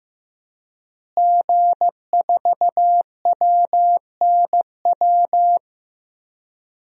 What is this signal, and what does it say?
Morse code, heard as a keyed tone, about 15 wpm, 700 Hz. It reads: G4WNW
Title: DXCC-100 Call Signs - 5 Characters - Repeated 2x - 15wpm